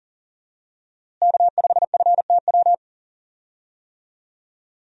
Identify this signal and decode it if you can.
Morse code, heard as a keyed tone, about 40 wpm, 700 Hz. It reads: K5FTW
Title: USA Call Signs - 5 Characters - Repeated 3x - 40wpm